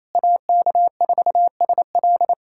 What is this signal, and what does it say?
Morse code, heard as a keyed tone, about 28 wpm, 700 Hz. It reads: AK4SL
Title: USA Callsign DE Callsign - 5 Characters - 28wpm